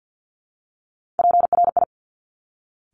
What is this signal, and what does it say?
Morse code, heard as a keyed tone, about 50 wpm, 700 Hz. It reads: PRI
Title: Top Three Letter Combinations in Words 50wpm